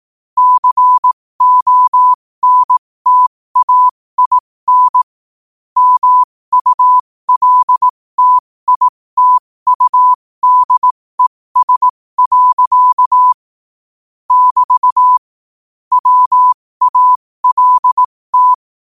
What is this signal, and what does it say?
Morse code, heard as a keyed tone, about 18 wpm, spaced as for 15 wpm, 1000 Hz. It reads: CONTAIN MULTITUDES. <BT> WALT
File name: qotdfiles / 15wpm